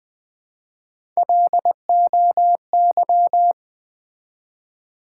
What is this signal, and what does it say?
Morse code, heard as a keyed tone, about 20 wpm, 700 Hz. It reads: LOY
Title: Build A Word - 1 Letter At A Time - 5-Letter Words - 20wpm